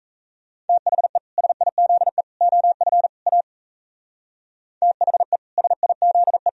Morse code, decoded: THE SIZE OF A THE SIZE

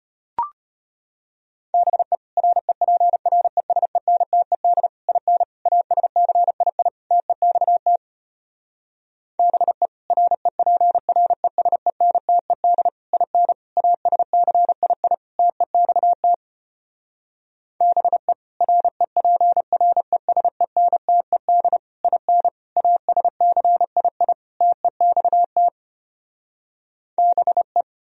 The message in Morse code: BE REPRESENTED IN ASCII TEXT BE REPRESENTED IN ASCII TEXT BE REPRESENTED IN ASCII TEXT BE

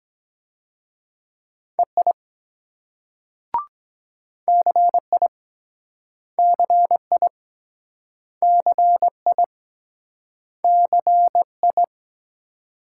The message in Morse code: EI CI CI CI CI

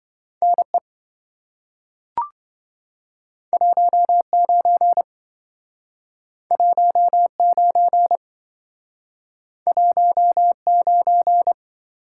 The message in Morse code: NE 19 19 19

